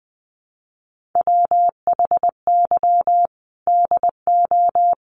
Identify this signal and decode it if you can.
Morse code, heard as a keyed tone, about 20 wpm, 700 Hz. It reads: WHY DO
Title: Sentences from Top 1400 Words 20wpm